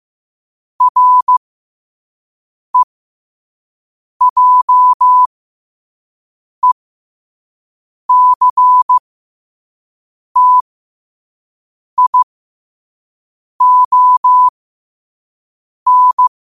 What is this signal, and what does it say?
Morse code, heard as a keyed tone, about 15 wpm, spaced as for 5 wpm, 1000 Hz. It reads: REJECTION
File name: qotdfiles / 5wpm